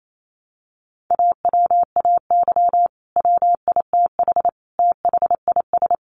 Morse code, heard as a keyed tone, about 28 wpm, 700 Hz. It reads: AWAY WITH THIS
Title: Sentences from Top 1100 Words 28wpm